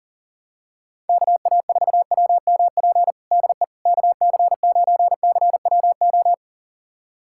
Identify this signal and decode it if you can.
Morse code, heard as a keyed tone, about 40 wpm, 700 Hz. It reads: KA4WMP DE KC9CWO
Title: USA Callsign DE Callsign - 6 Characters - 40wpm